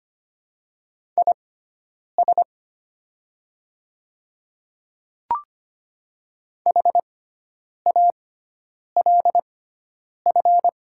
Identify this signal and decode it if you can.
Morse code, heard as a keyed tone, about 25 wpm, spaced as for 9 wpm, 700 Hz. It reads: IS HALF
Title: 6x Character Spacing - Top 500 Words - 25wpm